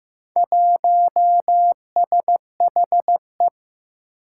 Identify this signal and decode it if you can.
Morse code, heard as a keyed tone, about 15 wpm, 700 Hz. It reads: 1SHE
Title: Four Challenging Letter-Number Combinations 15wpm